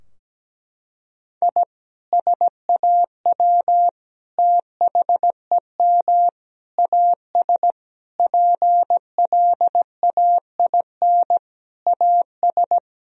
Morse code, heard as a keyed tone, about 17 wpm, 700 Hz. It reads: I SAW THEM AS PLAIN AS
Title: Sentences from Top 1000 Words - Review Part 2 - 17wpm